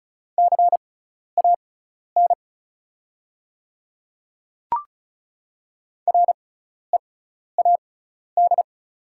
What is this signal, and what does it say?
Morse code, heard as a keyed tone, about 36 wpm, spaced as for 12 wpm, 700 Hz. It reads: CAN READ
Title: 6x Character Spacing - Top 300 Words - 35wpm